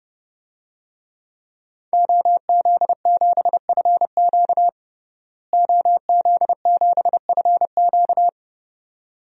Morse code, heard as a keyed tone, about 32 wpm, 700 Hz. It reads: OZ7FQ OZ7FQ
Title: DXCC-100 Call Signs - 5 Characters - Repeated 2x - 30wpm